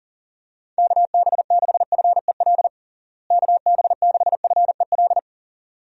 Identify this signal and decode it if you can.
Morse code, heard as a keyed tone, about 40 wpm, 700 Hz. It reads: KB6FEL KB6FEL
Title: USA Call Signs - 6 Characters - Repeated 2x - 40wpm